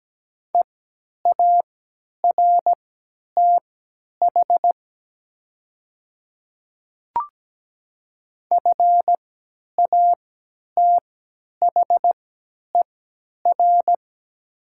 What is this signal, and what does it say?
Morse code, heard as a keyed tone, about 17 wpm, spaced as for 10 wpm, 700 Hz. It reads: EARTH FATHER
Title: 3x Character Spacing - Top 200 Words - 17wpm